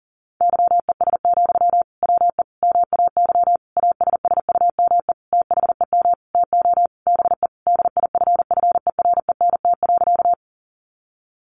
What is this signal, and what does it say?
Morse code, heard as a keyed tone, about 40 wpm, 700 Hz. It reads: YES, WE MAY ASSUME THEM TO BE DIFFERENT.
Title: Sentences from Top 1500 Words 40wpm